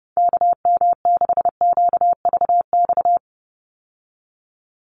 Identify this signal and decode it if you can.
Morse code, heard as a keyed tone, about 30 wpm, 700 Hz. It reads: KM6QVX
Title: USA Call Signs 30wpm